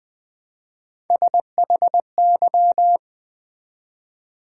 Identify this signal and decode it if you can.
Morse code, heard as a keyed tone, about 20 wpm, 700 Hz. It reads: SHY
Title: Build A Word - 1 Letter At A Time - 3-Letter Words - 20wpm